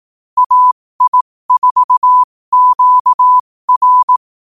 Morse code, heard as a keyed tone, about 18 wpm, spaced as for 15 wpm, 1000 Hz. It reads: AI4QR
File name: qotdfiles / 15wpm